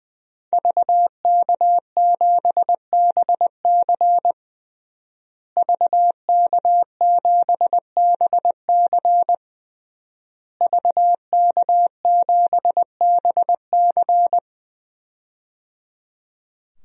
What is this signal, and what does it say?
Morse code, heard as a keyed tone, about 20 wpm, 700 Hz. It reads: VK7BC VK7BC VK7BC
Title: DXCC-100 Call Signs - 5 Characters - Repeated 3x - 20wpm